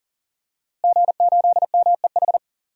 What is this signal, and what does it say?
Morse code, heard as a keyed tone, about 40 wpm, 700 Hz. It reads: G8MEH
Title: DXCC-100 Call Signs - 5 Characters - Repeated 3x - 40wpm